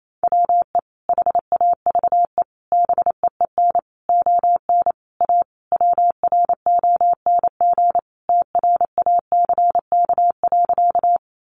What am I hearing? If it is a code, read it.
Morse code, heard as a keyed tone, about 28 wpm, 700 Hz. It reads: WE HAVE BEEN ON A WRONG TRACK.
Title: Sentences from Top 1100 Words 28wpm